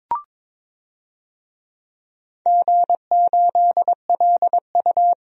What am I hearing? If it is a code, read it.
Morse code, heard as a keyed tone, about 21 wpm, 700 Hz. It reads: G8LU
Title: DXCC-100 Call Signs - 4 Characters - 22wpm